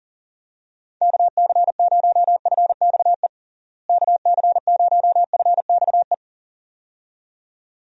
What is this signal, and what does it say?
Morse code, heard as a keyed tone, about 40 wpm, 700 Hz. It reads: KC0FXE KC0FXE
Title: USA Call Signs - 6 Characters - Repeated 2x - 40wpm